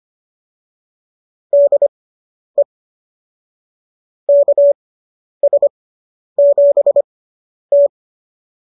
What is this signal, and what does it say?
Morse code, heard as a keyed tone, about 25 wpm, spaced as for 10 wpm, 590 Hz. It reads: DE KS7T